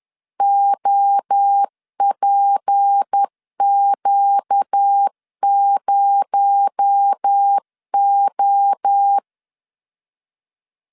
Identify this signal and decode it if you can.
Morse code, heard as a keyed tone, about 10 wpm, 800 Hz. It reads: OPQ0O
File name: CW / DET / 1 6 1500C S B 10wpm 4137